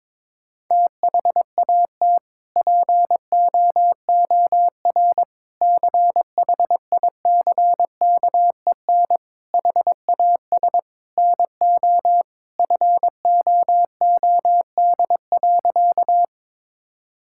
Morse code, heard as a keyed tone, about 22 wpm, 700 Hz. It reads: THAT POOR CHICKEN HAS NO FOOD.
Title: Sentences from Top 1000 Words - Review Part 5 - 22wpm